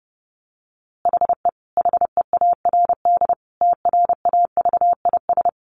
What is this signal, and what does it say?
Morse code, heard as a keyed tone, about 30 wpm, 700 Hz. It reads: HE HEARD TRAVIS
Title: A Sound Of Thunder 30wpm